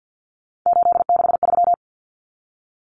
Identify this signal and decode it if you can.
Morse code, heard as a keyed tone, about 48 wpm, 700 Hz. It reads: 863
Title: Common Cut Numbers - 3-Digit 50wpm